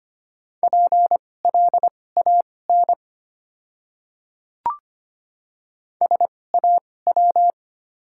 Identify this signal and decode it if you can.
Morse code, heard as a keyed tone, about 25 wpm, spaced as for 18 wpm, 700 Hz. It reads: PLAN SAW
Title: 2x Character Spacing - Top 500 Words - 25wpm